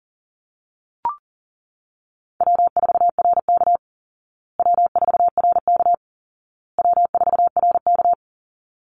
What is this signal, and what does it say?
Morse code, heard as a keyed tone, about 40 wpm, 700 Hz. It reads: W4RK W4RK W4RK
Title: Easy Call Signs Repeated 3x - 40wpm